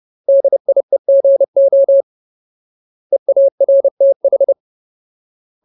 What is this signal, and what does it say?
Morse code, heard as a keyed tone, about 30 wpm, 550 Hz. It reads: DIEGO EARTH
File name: Ultimate Word List / 30WPM Ultimate